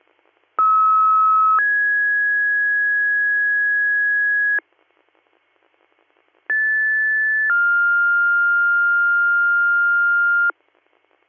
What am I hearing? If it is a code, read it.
Two-tone page, heard as a one-second tone, then a three-second tone, about 1290 then 1650 Hz, about 1750 then 1400 Hz. It is heard from 0.6 to 10.5 s.